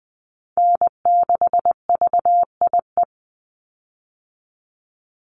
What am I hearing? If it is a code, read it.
Morse code, heard as a keyed tone, about 20 wpm, 700 Hz. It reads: N6VIE
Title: USA Call Signs 20wpm